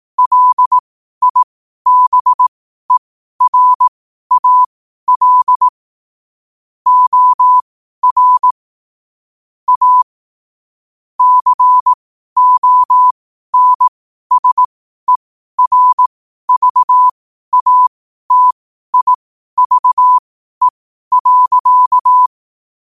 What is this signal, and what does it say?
Morse code, heard as a keyed tone, about 18 wpm, spaced as for 12 wpm, 1000 Hz. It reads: LIBERAL OR A CONSERVATIVE.
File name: qotdfiles / 12wpm